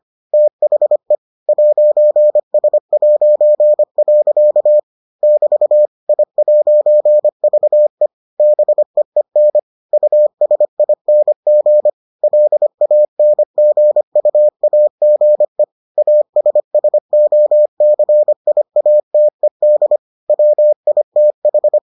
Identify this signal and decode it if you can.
Morse code, heard as a keyed tone, about 25 wpm, 600 Hz. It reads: THE 'S'. <BT> I'VE BEEN USING LANGUAGE ASSOCIATED WITH